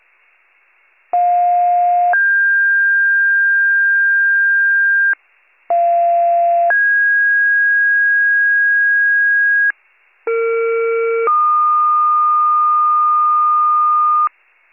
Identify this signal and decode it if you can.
Two-tone page, heard as a one-second tone, then a three-second tone, about 710 then 1600 Hz, about 690 then 1650 Hz, about 460 then 1220 Hz, with noise. It is heard from 1.1 to 14.3 s.